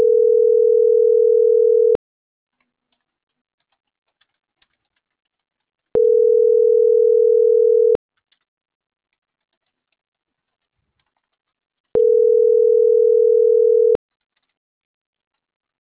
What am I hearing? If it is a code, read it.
Ringback tone (North America), 440 + 480 Hz.